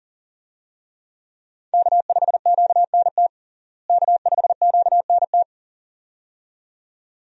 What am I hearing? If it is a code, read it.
Morse code, heard as a keyed tone, about 37 wpm, 700 Hz. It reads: K5QNT K5QNT
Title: USA Call Signs - 5 Characters - Repeated 2x - 40wpm